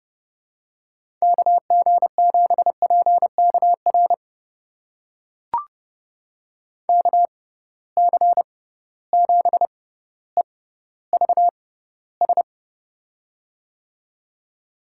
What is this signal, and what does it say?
Morse code, heard as a keyed tone, about 30 wpm, spaced as for 15 wpm, 700 Hz. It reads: KG7PKR KC7EVS
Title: USA Call Signs - 6 Characters - 6x Character Spacing - 30wpm